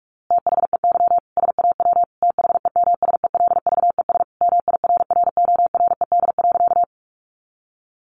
Morse code, heard as a keyed tone, about 44 wpm, 700 Hz. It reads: THEY SAW THEMSELVES MIRRORED.
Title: A Sound Of Thunder 45wpm